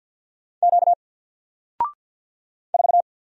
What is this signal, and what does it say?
Morse code, heard as a keyed tone, about 48 wpm, 700 Hz. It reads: Q 4
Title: Single Character (Letter:Number:Punctuation:Prosign) Alt 50wpm